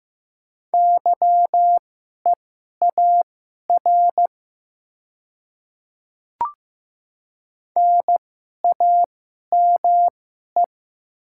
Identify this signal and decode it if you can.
Morse code, heard as a keyed tone, about 15 wpm, spaced as for 11 wpm, 700 Hz. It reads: YEAR NAME
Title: 2x Character Spacing - Top 200 Words - 15wpm